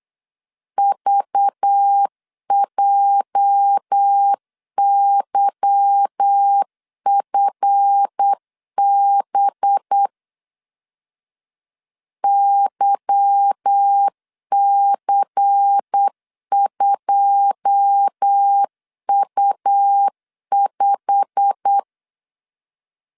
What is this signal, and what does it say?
Morse code, heard as a keyed tone, about 8 wpm, 800 Hz. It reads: VJYFB YC2U5
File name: LEZIONI / 9 2 8wpm 3437